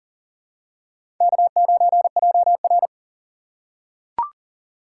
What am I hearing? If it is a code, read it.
Morse code, heard as a keyed tone, about 40 wpm, 700 Hz. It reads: K9JR
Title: USA Call Signs - 4 Characters - 6x Character Spacing - 40wpm